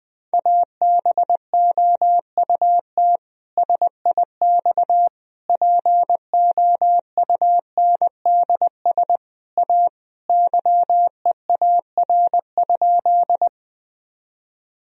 Morse code, heard as a keyed tone, about 20 wpm, 700 Hz. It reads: ABOUT SIX POUNDS A YEAR?
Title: Sentences from Top 1000 Words - Review Part 2 - 20wpm